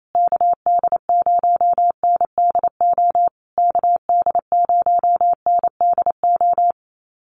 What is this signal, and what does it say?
Morse code, heard as a keyed tone, about 28 wpm, 700 Hz. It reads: KD0NDO KD0NDO
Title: USA Call Signs Repeated 3x 28wpm